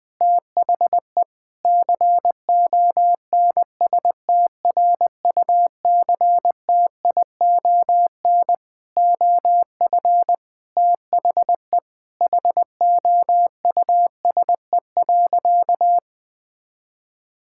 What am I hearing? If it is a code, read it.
Morse code, heard as a keyed tone, about 20 wpm, 700 Hz. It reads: THE CONSTRUCTION OF THE HOUSE.
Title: Sentences from Top 1600 Words 20wpm